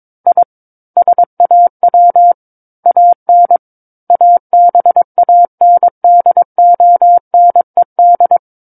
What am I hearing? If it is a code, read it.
Morse code, heard as a keyed tone, about 22 wpm, 700 Hz. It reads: I SAW AN ABANDONED
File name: Book One / 13-Chapter 13 - 22 WPM